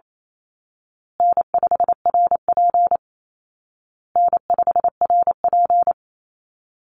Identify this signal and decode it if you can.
Morse code, heard as a keyed tone, about 28 wpm, 700 Hz. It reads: N5RP N5RP
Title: Easy Call Signs Repeated 3x - 28wpm